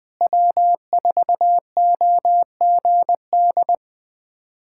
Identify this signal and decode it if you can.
Morse code, heard as a keyed tone, about 20 wpm, 700 Hz. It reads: W4OGD
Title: USA Callsign DE Callsign - 5 Characters - 20wpm